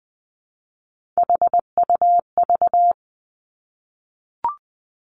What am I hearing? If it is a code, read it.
Morse code, heard as a keyed tone, about 19 wpm, 700 Hz. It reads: HUV
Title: Three Challenging Letter-Number Combinations 20wpm